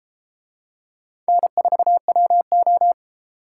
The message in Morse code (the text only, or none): N4WO